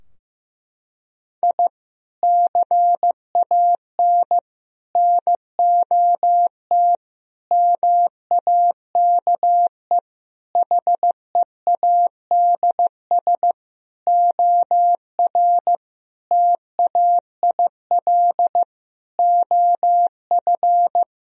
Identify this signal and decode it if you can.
Morse code, heard as a keyed tone, about 15 wpm, 700 Hz. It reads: I CAN NOT MAKE HEADS OR TAIL OF
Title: Sentences from Top 1000 Words - Review Part 3 - 15wpm